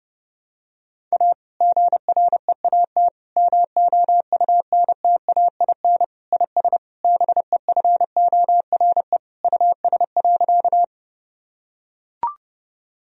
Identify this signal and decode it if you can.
Morse code, heard as a keyed tone, about 32 wpm, 700 Hz. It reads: A GREAT MOUNTAIN IS BEFORE US.
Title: Sentences from Top 1000 Words - Review Part 1 - 30wpm